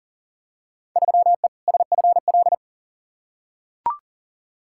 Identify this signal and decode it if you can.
Morse code, heard as a keyed tone, about 40 wpm, 700 Hz. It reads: 3E SFL
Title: Field Day Exchanges Encore - 40wpm